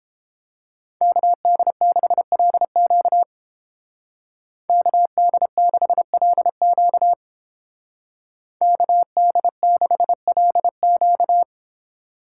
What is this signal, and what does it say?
Morse code, heard as a keyed tone, about 29 wpm, 700 Hz. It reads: KD6LQ KD6LQ KD6LQ